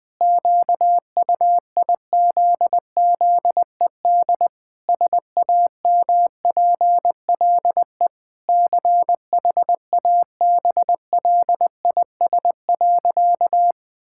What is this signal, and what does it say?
Morse code, heard as a keyed tone, about 19 wpm, 700 Hz. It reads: QUIZZED SAMPLE CHABLIS.
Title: Pangram Challenge v2 - 20wpm